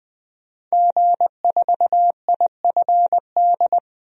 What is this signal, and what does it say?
Morse code, heard as a keyed tone, about 21 wpm, 700 Hz. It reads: G4IFD
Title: DXCC-100 Call Signs - 5 Characters - 20wpm